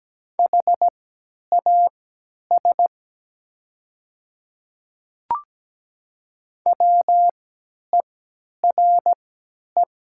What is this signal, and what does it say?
Morse code, heard as a keyed tone, about 17 wpm, spaced as for 10 wpm, 700 Hz. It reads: HAS WERE
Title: 3x Character Spacing - Top 300 Words - 17wpm